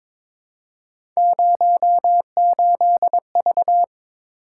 Morse code, heard as a keyed tone, about 22 wpm, 700 Hz. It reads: 08V
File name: practice / 22wpm / Three Letter-Number Combinations - 22wpm